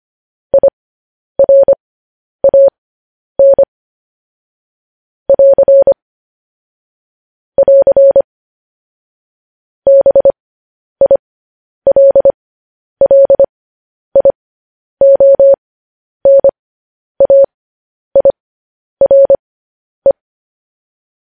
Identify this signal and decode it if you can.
Morse code, heard as a keyed tone, about 25 wpm, spaced as for 10 wpm, 550 Hz. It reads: IRAN <AR> <AR> BILLIONAIRE
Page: news headlines in morse code at 10 wpm